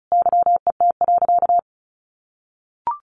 Morse code, heard as a keyed tone, about 34 wpm, 700 Hz. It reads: YET.
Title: A Sound Of Thunder 35wpm